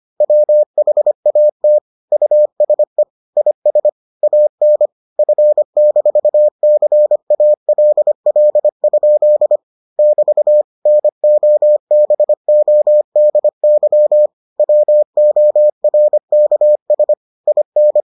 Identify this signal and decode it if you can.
Morse code, heard as a keyed tone, about 25 wpm, 600 Hz. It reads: WHAT USE IS AN F-CALL? <BT> NOBODY WORKS IN